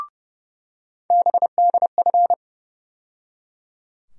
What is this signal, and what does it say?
Morse code, heard as a keyed tone, about 31 wpm, 700 Hz. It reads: BDF
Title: USA Call Sign Postfixes - 3 Letters 30wpm